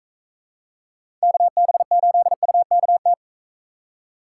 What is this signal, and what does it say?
Morse code, heard as a keyed tone, about 44 wpm, 700 Hz. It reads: KB8UKT